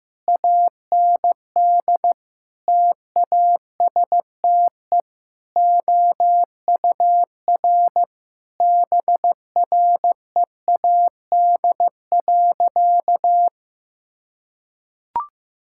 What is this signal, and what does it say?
Morse code, heard as a keyed tone, about 15 wpm, 700 Hz. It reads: AND TASTE OUR BREAD.
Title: Sentences from Top 1600 Words 15wpm